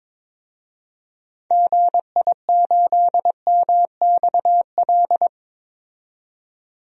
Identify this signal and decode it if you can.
Morse code, heard as a keyed tone, about 21 wpm, 700 Hz. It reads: GI8MXL
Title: DXCC-100 Call Signs - 6 Characters - 22wpm